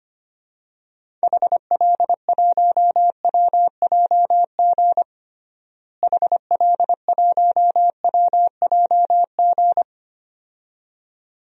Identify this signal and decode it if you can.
Morse code, heard as a keyed tone, about 25 wpm, 700 Hz. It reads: HL1WJG HL1WJG
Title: DXCC-100 Call Signs - 6 Characters - Repeated 2x - 25wpm